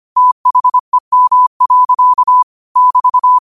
Morse code, heard as a keyed tone, about 25 wpm, 1000 Hz. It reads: THEM. <BT>
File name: qotdfiles / 25wpm